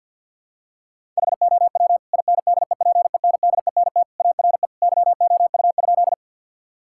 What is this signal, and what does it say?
Morse code, heard as a keyed tone, about 51 wpm, 700 Hz. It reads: HOW INDEPENDENT ARE YOU?
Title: Sentences from Top 1000 Words - Review Part 6 - 50wpm